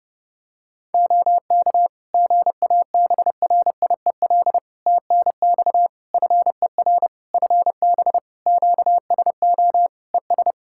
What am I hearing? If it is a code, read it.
Morse code, heard as a keyed tone, about 30 wpm, 700 Hz. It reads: OK GABRIEL TNX FER FB QSO ES
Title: Practice QSO - Exchange 4 - Ending - 30wpm